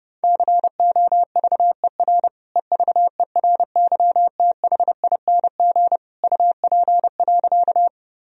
Morse code, heard as a keyed tone, about 32 wpm, 700 Hz. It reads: COVER EVERYTHING UP.
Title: Sentences from Top 1000 Words - Review Part 1 - 30wpm